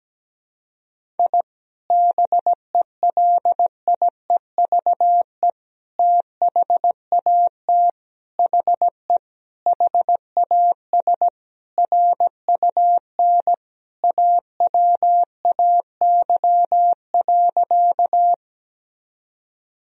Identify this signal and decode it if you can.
Morse code, heard as a keyed tone, about 17 wpm, 700 Hz. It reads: I BELIEVE THAT HE HAS RUN AWAY.
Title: Sentences from Top 1000 Words - Review Part 1 - 17wpm